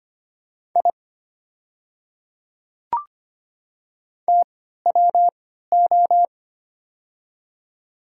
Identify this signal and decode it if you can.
Morse code, heard as a keyed tone, about 25 wpm, spaced as for 14 wpm, 690 Hz. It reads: I TWO
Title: 3x Character Spacing - Top 500 Words - 25wpm